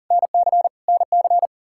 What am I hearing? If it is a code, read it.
Morse code, heard as a keyed tone, about 40 wpm, 700 Hz. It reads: NC NC
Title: K1USN SST QSO Practice 40wpm